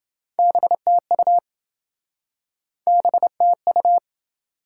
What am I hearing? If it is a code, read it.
Morse code, heard as a keyed tone, about 27 wpm, 700 Hz. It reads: BTU BTU